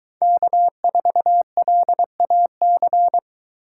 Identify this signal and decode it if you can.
Morse code, heard as a keyed tone, about 23 wpm, 700 Hz. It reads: K4LAC